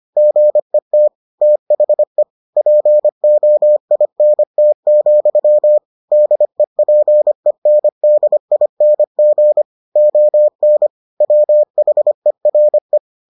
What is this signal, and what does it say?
Morse code, heard as a keyed tone, about 25 wpm, 600 Hz. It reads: GET THE POINT, DEPENDING ON WHERE